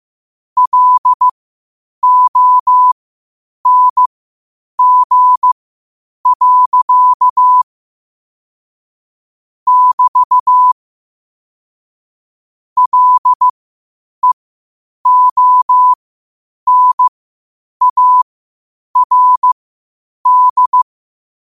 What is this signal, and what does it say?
Morse code, heard as a keyed tone, about 15 wpm, spaced as for 8 wpm, 1000 Hz. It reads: LONG. <BT> LEONARD
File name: qotdfiles / 8wpm